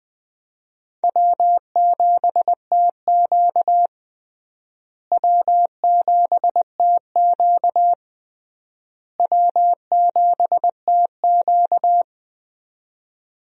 Morse code, heard as a keyed tone, about 20 wpm, 700 Hz. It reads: W7TQ W7TQ W7TQ
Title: USA Call Signs - 4 Characters - Repeated 3x - 20wpm